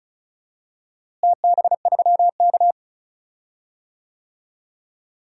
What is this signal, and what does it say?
Morse code, heard as a keyed tone, about 36 wpm, 700 Hz. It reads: TB3K
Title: DXCC-100 Call Signs - 4 Characters - Repeated 3x - 35wpm